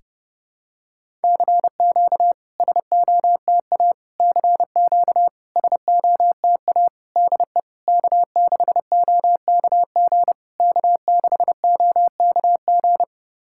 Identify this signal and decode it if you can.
Morse code, heard as a keyed tone, about 30 wpm, 700 Hz. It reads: CQ SOTA CQ SOTA DE K6OKG K6OKG
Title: SOTA QSO Practice 30wpm